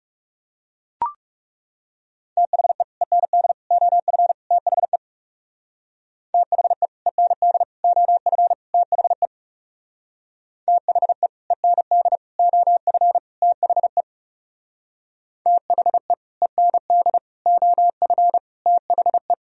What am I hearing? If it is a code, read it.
Morse code, heard as a keyed tone, about 38 wpm, 700 Hz. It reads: THE END OF THE THE END OF THE THE END OF THE THE END OF THE